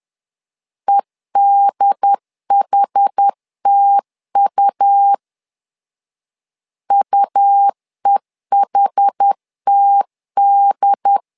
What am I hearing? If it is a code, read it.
Morse code, heard as a keyed tone, about 10 wpm, 800 Hz. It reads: EDHTU UEHTD
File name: LEZIONI / 2 UHDN / 2 UHD 10wpm 1826